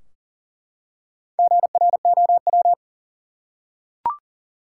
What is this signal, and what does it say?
Morse code, heard as a keyed tone, about 38 wpm, 700 Hz. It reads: GROW
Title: Four Letter Words - 40wpm